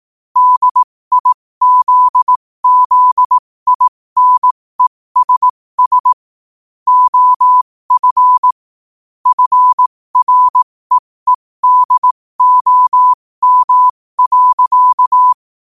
Morse code, heard as a keyed tone, about 18 wpm, spaced as for 15 wpm, 1000 Hz. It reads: DIZZINESS OF FREEDOM.
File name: qotdfiles / 15wpm